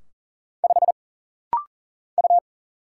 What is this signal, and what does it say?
Morse code, heard as a keyed tone, about 40 wpm, 700 Hz. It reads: 5 U
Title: Single Letter-Number - Rapid-Fire 40wpm